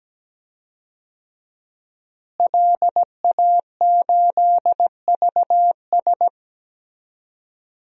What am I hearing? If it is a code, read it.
Morse code, heard as a keyed tone, about 17 wpm, 700 Hz. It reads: LA8VS
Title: DXCC-100 Call Signs - 5 Characters - Repeated 3x - 17wpm